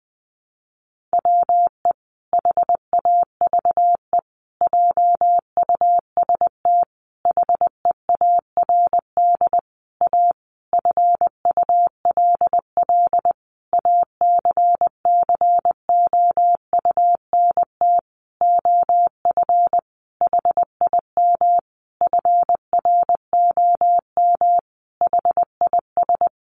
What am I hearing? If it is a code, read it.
Morse code, heard as a keyed tone, about 20 wpm, 700 Hz. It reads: WE HAVE JUST HEARD A FULL ACCOUNT OF HIM FROM HIS
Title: Sentences from Top 900 Words 20wpm